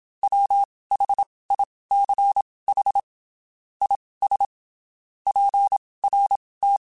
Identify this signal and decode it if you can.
Morse code, heard as a keyed tone, about 27 wpm, spaced as for 19 wpm, 800 Hz. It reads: WHICH IS PRT
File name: (spaced) / QSOs Spaced-mp3s / WA7QOX 1 @25 sp